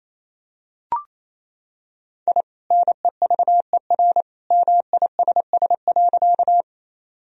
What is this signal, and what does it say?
Morse code, heard as a keyed tone, about 28 wpm, 700 Hz. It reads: I NEVER MISS.
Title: Sentences from Top 1000 Words - Review Part 3 - 28wpm